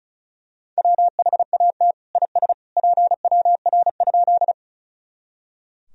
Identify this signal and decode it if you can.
Morse code, heard as a keyed tone, about 36 wpm, 700 Hz. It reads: WHAT IS PWR?
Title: Sentences from Top 500 Words and Abbreviations 35wpm